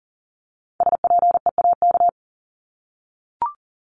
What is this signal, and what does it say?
Morse code, heard as a keyed tone, about 40 wpm, 700 Hz. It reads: SPEAK
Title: ING Suffix - Top 100 Words with Base Words 40wpm